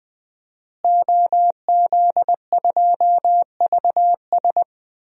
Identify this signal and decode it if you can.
Morse code, heard as a keyed tone, about 21 wpm, 700 Hz. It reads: OZ2VS
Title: DXCC-100 Call Signs - 5 Characters - Repeated 2x - 20wpm